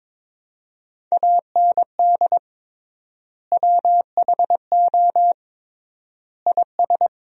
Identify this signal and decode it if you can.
Morse code, heard as a keyed tone, about 21 wpm, 700 Hz. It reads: AND WHO IS
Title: 3x Word-Spacing - Sentences from Top 200 Words - 22wpm